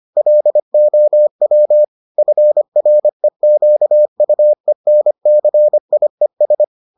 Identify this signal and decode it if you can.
Morse code, heard as a keyed tone, about 25 wpm, 600 Hz. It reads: LOW FREQUENCIES